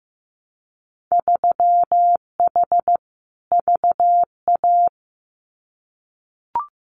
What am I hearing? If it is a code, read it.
Morse code, heard as a keyed tone, about 15 wpm, 700 Hz. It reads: 3H VA
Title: Winter Field Day Exchanges 15wpm